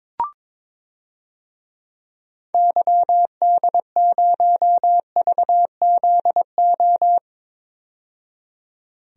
Morse code, heard as a keyed tone, about 21 wpm, 700 Hz. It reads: YD0VZO
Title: DXCC-100 Call Signs - 6 Characters - 22wpm